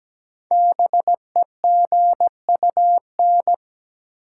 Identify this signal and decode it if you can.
Morse code, heard as a keyed tone, about 17 wpm, 700 Hz. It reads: BEGUN
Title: Sentences from Top 1000 Words - Review Part 6 - New Word Drill - 17wpm